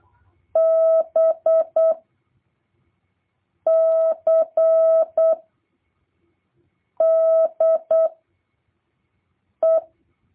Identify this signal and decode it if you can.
Morse code, heard as a keyed tone, about 8 wpm, 650 Hz. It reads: B C D E
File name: jh8cbh / 2023cw